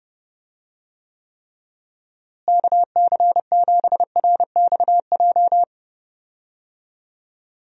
Morse code, heard as a keyed tone, about 30 wpm, 700 Hz. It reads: KC7RXJ
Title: USA Call Signs - 6 Characters - Repeated 3x - 30wpm